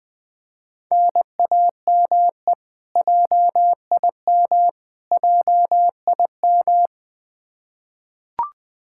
Morse code, heard as a keyed tone, about 20 wpm, 700 Hz. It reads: NAME JIM JIM
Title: Practice QSO - Exchange 1 - Beginning - 20wpm